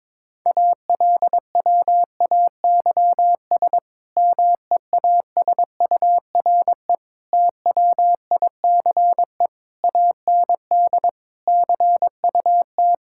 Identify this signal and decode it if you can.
Morse code, heard as a keyed tone, about 22 wpm, 700 Hz. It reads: ALWAYS MEASURE TWICE AND CUT
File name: Difficult Sentences with Hints - 22wpm